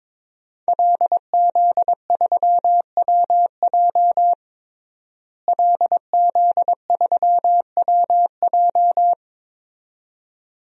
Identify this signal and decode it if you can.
Morse code, heard as a keyed tone, about 22 wpm, 700 Hz. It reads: LZ3WJ LZ3WJ
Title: DXCC-100 Call Signs - 5 Characters - Repeated 2x - 22wpm